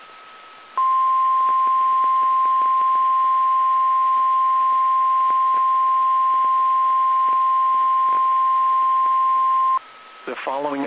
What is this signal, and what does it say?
Emergency Alert System (SAME) tones (the 1050 Hz attention tone), with speech. It is heard from 0.7 to 9.8 s.
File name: eas / KSDZKDJL / KSDZKDJL